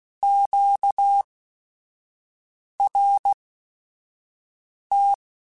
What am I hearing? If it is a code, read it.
Morse code, heard as a keyed tone, about 16 wpm, spaced as for 5 wpm, 800 Hz. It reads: QRT
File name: Q Calls / Q Calls-mp3s / Q Calls Twice @5